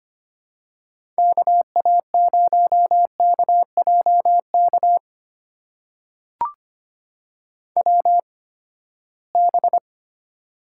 Morse code, heard as a keyed tone, about 25 wpm, spaced as for 17 wpm, 700 Hz. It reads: KA0KJK WB
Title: USA Call Signs - 6 Characters - 8x Character Spacing - 25wpm